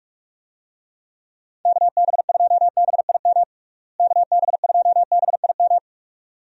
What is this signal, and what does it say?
Morse code, heard as a keyed tone, about 42 wpm, 700 Hz. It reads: KB2BIM KB2BIM
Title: USA Call Signs - 6 Characters - Repeated 2x - 45wpm